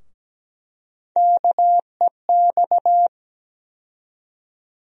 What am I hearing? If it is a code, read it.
Morse code, heard as a keyed tone, about 17 wpm, 700 Hz. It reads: KEX
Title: USA Call Sign Postfixes - 3 Letters 17wpm